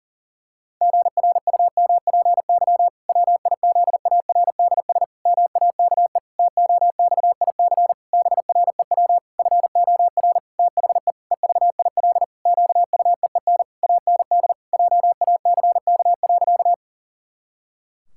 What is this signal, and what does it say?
Morse code, heard as a keyed tone, about 40 wpm, 700 Hz. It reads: GRUMPY WIZARDS MAKE TOXIC BREW FOR THE EVIL QUEEN AND JACK.
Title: Pangram Challenge v2 - 40wpm